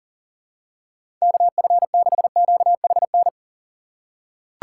Morse code, heard as a keyed tone, about 40 wpm, 700 Hz. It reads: KF6QHN
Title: USA Call Signs - 4x Character Spacing - 40wpm